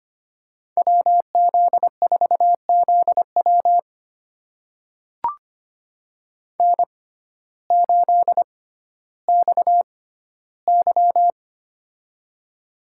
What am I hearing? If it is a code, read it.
Morse code, heard as a keyed tone, about 25 wpm, spaced as for 9 wpm, 700 Hz. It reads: WZ4Z WN8XY
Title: USA Call Signs - 6x Character Spacing - 25wpm